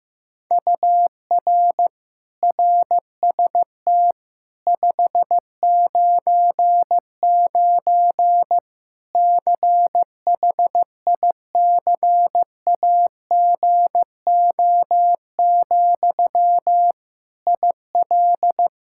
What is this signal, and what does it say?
Morse code, heard as a keyed tone, about 15 wpm, 700 Hz. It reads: UR RST 599 CHICAGO, IL
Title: Practice QSO - Exchange 1 - Beginning - 15wpm